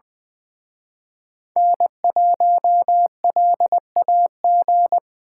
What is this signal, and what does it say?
Morse code, heard as a keyed tone, about 23 wpm, 700 Hz. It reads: N1LAG
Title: Speed Racing - USA 1x3 Call Signs 20wpm